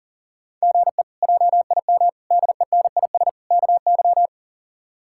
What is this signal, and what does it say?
Morse code, heard as a keyed tone, about 40 wpm, 700 Hz. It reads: GE JIM DENIS KY
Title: K1USN SST QSO Practice - 40wpm